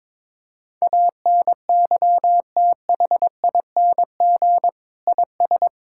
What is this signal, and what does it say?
Morse code, heard as a keyed tone, about 22 wpm, 700 Hz. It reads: ANYTHING IS
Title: Sentences from Top 1000 Words - Review Part 6 - 22wpm